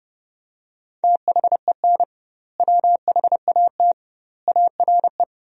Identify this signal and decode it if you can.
Morse code, heard as a keyed tone, about 30 wpm, 700 Hz. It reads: THEN WHAT ARE
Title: 2x Word-Spacing - Sentences from Top 100 Words - 30wpm